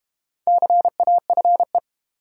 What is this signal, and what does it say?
Morse code, heard as a keyed tone, about 32 wpm, 700 Hz. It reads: CAFE